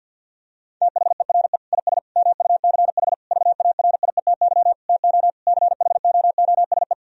Morse code, heard as a keyed tone, about 50 wpm, 700 Hz. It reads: THERE IS MUCH VARIETY TO CHOOSE
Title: Sentences from Top 1000 Words - Review Part 4 - 50wpm